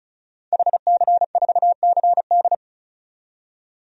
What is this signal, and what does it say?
Morse code, heard as a keyed tone, about 34 wpm, 700 Hz. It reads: HC4CD
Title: DXCC-100 Call Signs - 5 Characters - 35wpm